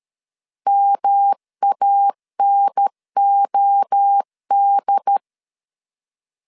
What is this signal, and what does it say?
Morse code, heard as a keyed tone, about 13 wpm, 800 Hz. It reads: MANOD
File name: CW / DET / 1 3 1500C S B 12wpm 2828